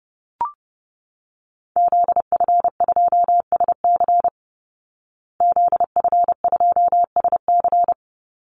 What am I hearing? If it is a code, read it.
Morse code, heard as a keyed tone, about 30 wpm, 700 Hz. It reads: ZF2SC ZF2SC
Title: Call Signs Repeated 3x - 30wpm